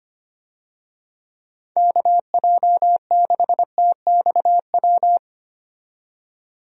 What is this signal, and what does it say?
Morse code, heard as a keyed tone, about 25 wpm, 700 Hz. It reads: KJ6TXW
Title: USA Call Signs - 6 Characters - 25wpm